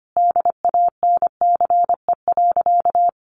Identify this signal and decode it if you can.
Morse code, heard as a keyed tone, about 25 wpm, 700 Hz. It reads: DANCE.
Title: Sentences from Top 700 Words 25wpm